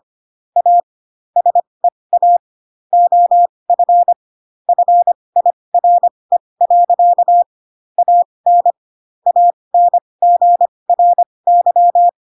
Morse code, heard as a keyed tone, about 25 wpm, spaced as for 20 wpm, 700 Hz. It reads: A SEA OF FIRE. AN ANGRY
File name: Pinocchio / Pinocchio ch6 15 25 ik1hge